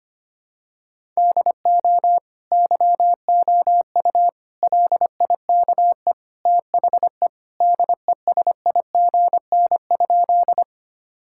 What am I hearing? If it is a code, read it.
Morse code, heard as a keyed tone, about 25 wpm, 700 Hz. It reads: DO YOU LIKE THE DESIGN?
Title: Sentences from Top 1000 Words - Review Part 5 - 25wpm